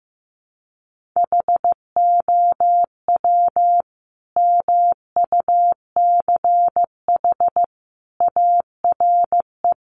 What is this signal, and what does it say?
Morse code, heard as a keyed tone, about 15 wpm, 700 Hz. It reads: HOW MUCH ARE
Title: Sentences from Top 200 Words 15wpm